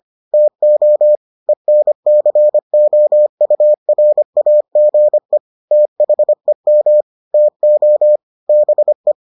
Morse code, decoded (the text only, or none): TO ENCOURAGE THEM TO BE